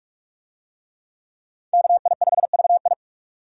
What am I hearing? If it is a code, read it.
Morse code, heard as a keyed tone, about 42 wpm, 700 Hz. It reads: KI5VI